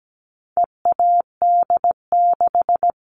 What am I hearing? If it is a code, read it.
Morse code, heard as a keyed tone, about 17 wpm, 700 Hz. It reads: EAD6
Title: All Cut Numbers - 4-Digit 17wpm